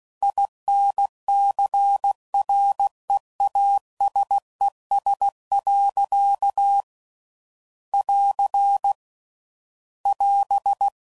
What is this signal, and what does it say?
Morse code, heard as a keyed tone, about 16 wpm, 800 Hz. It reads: INCREASES. <AR> <AS>
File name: CW Code Groups @15WPM / Technical Subjects / XC